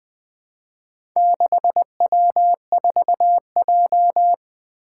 Morse code, heard as a keyed tone, about 20 wpm, 700 Hz. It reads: 6W4J
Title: Four Letter-Number Combinations - 20wpm